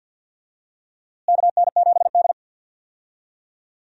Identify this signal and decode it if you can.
Morse code, heard as a keyed tone, about 50 wpm, 700 Hz. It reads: KN7D
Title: USA Call Signs - 4 Characters - Repeated 2x - 50wpm